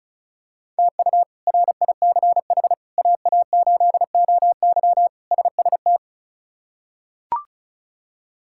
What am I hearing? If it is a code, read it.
Morse code, heard as a keyed tone, about 34 wpm, 700 Hz. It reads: TU RICH AA8OY SST
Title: K1USN SST QSO Practice - 35wpm